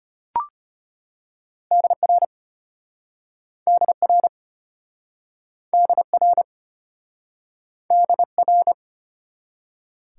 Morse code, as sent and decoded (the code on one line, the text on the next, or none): -.. .-. / -.. .-. / -.. .-. / -.. .-.
DR DR DR DR